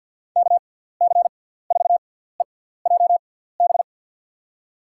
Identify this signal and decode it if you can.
Morse code, heard as a keyed tone, about 52 wpm, spaced as for 17 wpm, 700 Hz. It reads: KC4EJB
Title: USA Call Signs - 6x Character Spacing - 50wpm